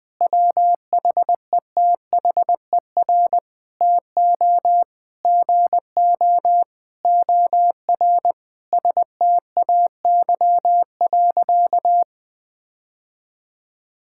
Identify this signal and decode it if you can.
Morse code, heard as a keyed tone, about 20 wpm, 700 Hz. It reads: WHETHER TO GO OR STAY.
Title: Sentences with 5-Word Phrases - 20wpm